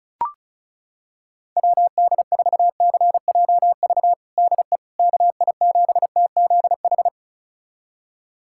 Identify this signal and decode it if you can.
Morse code, heard as a keyed tone, about 34 wpm, 700 Hz. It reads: WD4CJV DE KI7TZH
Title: USA Callsign DE Callsign - 6 Characters - 35wpm